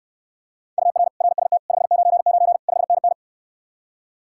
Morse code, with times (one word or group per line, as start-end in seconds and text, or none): 0.8-1.1 UR
1.2-1.6 RST
1.7-2.6 599
2.7-3.1 5NN